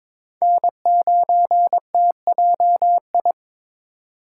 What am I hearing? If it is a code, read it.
Morse code, heard as a keyed tone, about 26 wpm, 700 Hz. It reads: N9TJI
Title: Speed Racing - USA Call Signs 22wpm